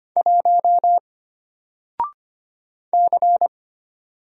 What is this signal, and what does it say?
Morse code, heard as a keyed tone, about 25 wpm, 700 Hz. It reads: 1 C
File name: Single-Letter-Number-25wpm